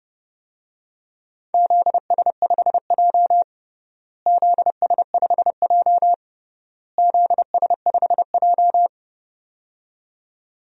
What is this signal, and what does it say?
Morse code, heard as a keyed tone, about 31 wpm, 700 Hz. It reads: ZS5J ZS5J ZS5J
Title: DXCC-100 Call Signs - 4 Characters - Repeated 3x - 30wpm